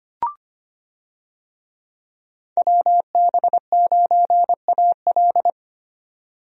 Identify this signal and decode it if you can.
Morse code, heard as a keyed tone, about 25 wpm, 700 Hz. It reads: WB9AL
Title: USA Call Signs - 5 Characters - 25wpm